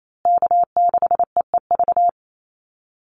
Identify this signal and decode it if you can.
Morse code, heard as a keyed tone, about 28 wpm, 700 Hz. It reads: K6EEV
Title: USA Call Signs Repeated 2x 28wpm